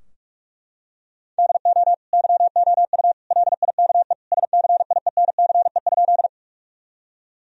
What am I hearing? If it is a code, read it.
Morse code, heard as a keyed tone, about 43 wpm, 700 Hz. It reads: DO YOU LIKE SCIENCE?
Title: Sentences from Top 1000 Words - Review Part 2 - 45wpm